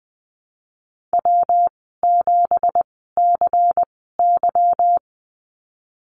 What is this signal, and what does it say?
Morse code, heard as a keyed tone, about 20 wpm, spaced as for 15 wpm, 700 Hz. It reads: W7CY
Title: 173 Lesson 28g - Number 9 - Call Signs 20wpm